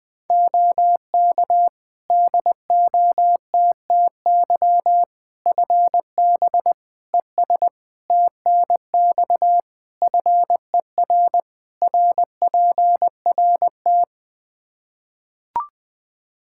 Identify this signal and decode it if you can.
Morse code, heard as a keyed tone, about 20 wpm, 700 Hz. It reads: OK DOTTY FB ES TNX FER RPRT
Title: K1USN SST QSO Practice 20wpm